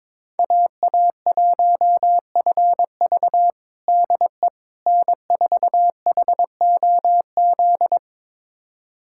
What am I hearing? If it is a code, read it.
Morse code, heard as a keyed tone, about 22 wpm, 700 Hz. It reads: AA1FV DE N4HOZ
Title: USA Callsign DE Callsign - 5 Characters - 22wpm